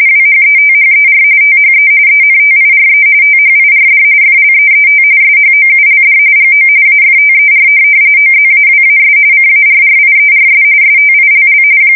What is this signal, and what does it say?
RTTY, about 45.45 baud. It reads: WHETHER THE ENTIRE THIRD FLEET SHOULD GO AFTER OZAWAS CARRIERS STILL
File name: fsk170 / USS-New-Jersey